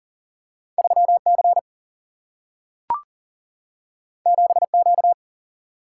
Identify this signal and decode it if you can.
Morse code, heard as a keyed tone, about 40 wpm, 700 Hz. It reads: 3C 7Q
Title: Single-Letter Single-Number 40wpm